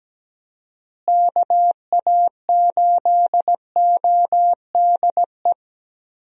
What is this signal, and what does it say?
Morse code, heard as a keyed tone, about 17 wpm, 700 Hz. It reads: KA8ODE
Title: USA Call Signs - 6 Characters - Repeated 2x - 17wpm